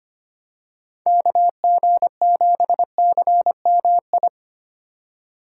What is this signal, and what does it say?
Morse code, heard as a keyed tone, about 25 wpm, 700 Hz. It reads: KG7CMI
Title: USA Call Signs - 6 Characters - 4x Character Spacing - 25wpm